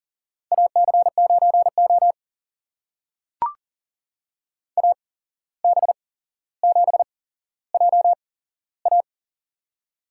Morse code, heard as a keyed tone, about 40 wpm, spaced as for 16 wpm, 700 Hz. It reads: AC9O AB7JA
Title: USA Call Signs - 8x Character Spacing - 40wpm